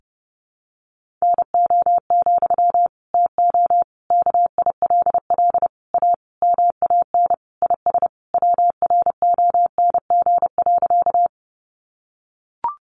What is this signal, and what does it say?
Morse code, heard as a keyed tone, about 30 wpm, 700 Hz. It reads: NO, TO KILL A MAN IS WRONG.
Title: Sentences from Top 1400 Words 30wpm